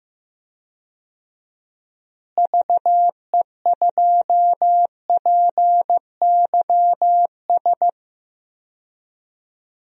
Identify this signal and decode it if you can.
Morse code, heard as a keyed tone, about 15 wpm, 700 Hz. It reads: VE2PYS
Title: DXCC-100 Call Signs - 6 Characters - Repeated 3x - 15wpm